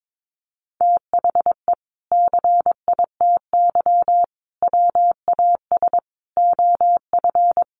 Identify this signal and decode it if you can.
Morse code, heard as a keyed tone, about 22 wpm, 700 Hz. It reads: THE CITY WAS OF
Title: Sentences from Top 1500 Words 22wpm